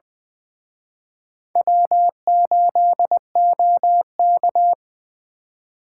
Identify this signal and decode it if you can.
Morse code, heard as a keyed tone, about 23 wpm, 700 Hz. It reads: W8OK
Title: Speed Racing - USA Call Signs 20wpm